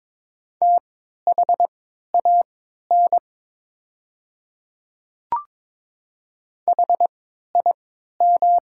Morse code, heard as a keyed tone, about 22 wpm, spaced as for 12 wpm, 700 Hz. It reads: THAN HIM
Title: 3x Character Spacing - Top 100 Words - 22wpm